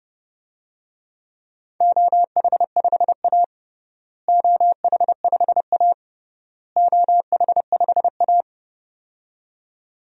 Morse code, heard as a keyed tone, about 32 wpm, 700 Hz. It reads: OH5A OH5A OH5A
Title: DXCC-100 Call Signs - 4 Characters - Repeated 3x - 30wpm